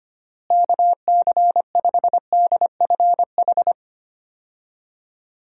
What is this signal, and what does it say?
Morse code, heard as a keyed tone, about 25 wpm, 700 Hz. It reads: KC5DFH
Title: USA Call Signs - 6 Characters - Repeated 3x - 25wpm